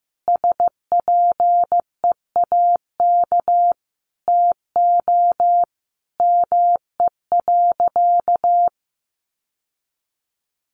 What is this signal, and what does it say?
Morse code, heard as a keyed tone, about 15 wpm, 700 Hz. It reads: SPEAK TO ME.
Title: Sentences from Top 1100 Words 15wpm